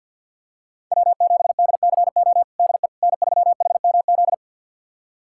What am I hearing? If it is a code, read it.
Morse code, heard as a keyed tone, about 51 wpm, 700 Hz. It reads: W7DCO DE N3HMZ
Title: USA Callsign DE Callsign - 5 Characters - 50wpm